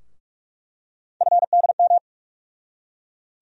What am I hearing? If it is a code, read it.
Morse code, heard as a keyed tone, about 43 wpm, 700 Hz. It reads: FDM
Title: USA Call Sign Postfixes - 3 Letters 45wpm